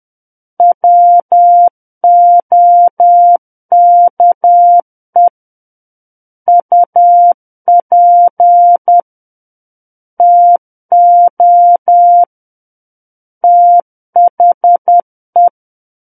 Morse code, heard as a keyed tone, about 10 wpm, 700 Hz. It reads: WOKE UP TO THE
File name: Book One / 01-Chapter 1 - 10 WPM